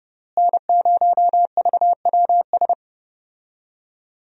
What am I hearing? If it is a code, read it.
Morse code, heard as a keyed tone, about 30 wpm, 700 Hz. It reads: N0VWS
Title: USA Call Signs - 5 Characters - Repeated 3x - 30wpm